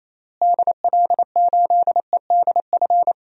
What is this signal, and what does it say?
Morse code, heard as a keyed tone, about 28 wpm, 700 Hz. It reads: DL8EDF
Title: DXCC-100 Call Signs - 6 Characters - Repeated 3x - 28wpm